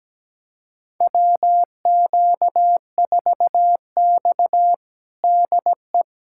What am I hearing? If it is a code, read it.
Morse code, heard as a keyed tone, about 17 wpm, 700 Hz. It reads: WQ4X DE
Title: USA Callsign DE Callsign - 4 Characters - 17wpm